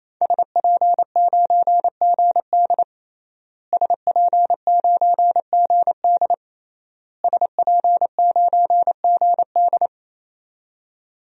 Morse code, heard as a keyed tone, about 28 wpm, 700 Hz. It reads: SP9GD SP9GD SP9GD
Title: DXCC-100 Call Signs - 5 Characters - Repeated 3x - 28wpm